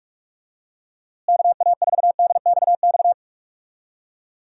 Morse code, heard as a keyed tone, about 46 wpm, 700 Hz. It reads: KA4DXX
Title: USA Call Signs - 6 Characters - Repeated 3x - 45wpm